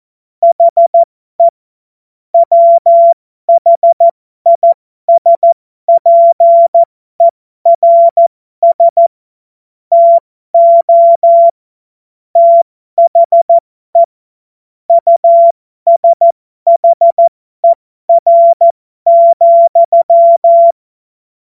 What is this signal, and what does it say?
Morse code, heard as a keyed tone, about 14 wpm, spaced as for 12 wpm, 680 Hz. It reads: HE WHISPERS TO THE USHER,
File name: Morsekurs / Witze